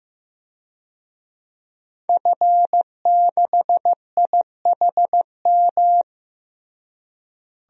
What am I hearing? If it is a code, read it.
Morse code, heard as a keyed tone, about 15 wpm, 700 Hz. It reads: F6IHM
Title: DXCC-100 Call Signs - 5 Characters - Repeated 3x - 15wpm